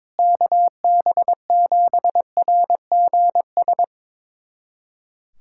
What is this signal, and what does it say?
Morse code, heard as a keyed tone, about 22 wpm, 700 Hz. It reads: KB7RGS
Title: USA Callsign DE Callsign - 6 Characters - 22wpm